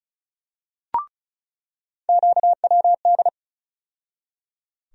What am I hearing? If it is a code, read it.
Morse code, heard as a keyed tone, about 34 wpm, 700 Hz. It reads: QWD
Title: USA Call Sign Postfixes - 3 Letters 35wpm